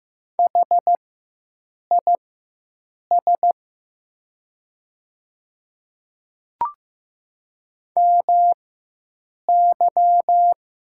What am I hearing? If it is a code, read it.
Morse code, heard as a keyed tone, about 15 wpm, spaced as for 7 wpm, 700 Hz. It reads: HIS MY
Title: 4x Character Spacing - Top 100 Words - 15wpm